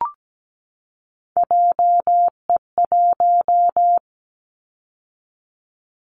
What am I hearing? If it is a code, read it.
Morse code, heard as a keyed tone, about 17 wpm, 700 Hz. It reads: JE1
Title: Callsign Prefixes for Top 25 DXCC - 17wpm